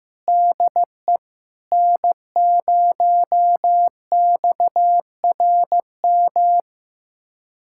Morse code, heard as a keyed tone, about 15 wpm, 700 Hz. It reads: DE N0XRM
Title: USA Callsign DE Callsign - 5 Characters - 15wpm